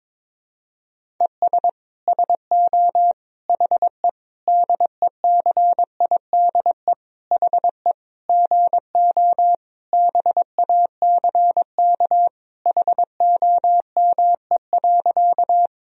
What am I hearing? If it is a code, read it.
Morse code, heard as a keyed tone, about 22 wpm, 700 Hz. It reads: ES SO HE DECIDE HE GO BACK HOME.
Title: Sentences from Top 500 Words and Abbreviations 22wpm